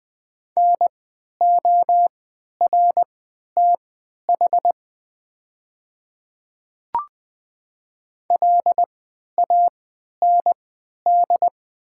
Morse code, heard as a keyed tone, about 20 wpm, spaced as for 11 wpm, 700 Hz. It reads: NORTH LAND
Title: 3x Character Spacing - Top 500 Words - 20wpm